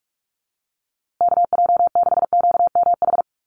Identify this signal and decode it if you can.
Morse code, heard as a keyed tone, about 44 wpm, 700 Hz. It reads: KJ6QMH
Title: USA Call Signs Repeated 2x 45wpm